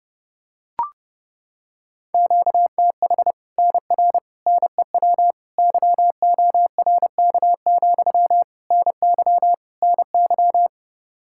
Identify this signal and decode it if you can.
Morse code, heard as a keyed tone, about 30 wpm, 700 Hz. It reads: QTH NR NEW YORK, NY NY
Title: Practice QSO - Exchange 1 - Beginning - 30wpm